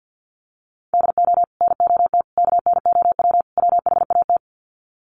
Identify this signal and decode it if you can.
Morse code, heard as a keyed tone, about 50 wpm, 700 Hz. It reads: DO NOT KNOW WHAT
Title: Sets of 4 Words 50wpm